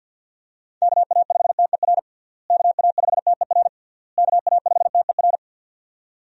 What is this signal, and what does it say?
Morse code, heard as a keyed tone, about 47 wpm, 700 Hz. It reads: KA5TER KA5TER KA5TER
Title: USA Call Signs - 6 Characters - Repeated 3x - 50wpm